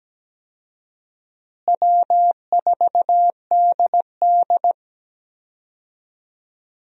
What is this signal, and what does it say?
Morse code, heard as a keyed tone, about 17 wpm, 700 Hz. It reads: W4DD
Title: USA Call Signs - 4 Characters - Repeated 3x - 17wpm